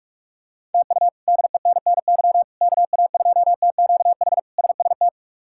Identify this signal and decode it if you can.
Morse code, heard as a keyed tone, about 45 wpm, 700 Hz. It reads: TU DENNY KA2TQH SST
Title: K1USN SST QSO Practice - 45wpm